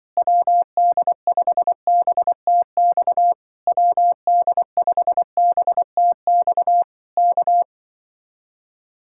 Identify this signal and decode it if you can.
Morse code, heard as a keyed tone, about 24 wpm, 700 Hz. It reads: WD5BTX WD5BTX K
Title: Speed-Racing Common QSO Phrases - 20wpm